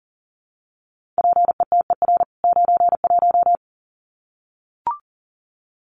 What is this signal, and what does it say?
Morse code, heard as a keyed tone, about 40 wpm, 700 Hz. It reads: PETER 91
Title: CWT Exchange 40wpm